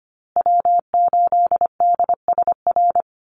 Morse code, heard as a keyed tone, about 25 wpm, 700 Hz. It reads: W8DSR
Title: USA Call Signs Repeated 3x 25wpm